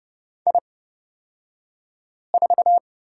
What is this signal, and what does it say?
Morse code, heard as a keyed tone, about 30 wpm, spaced as for 5 wpm, 700 Hz. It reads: I4
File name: Lesson 38d - Letter q - Two Characters 30wpm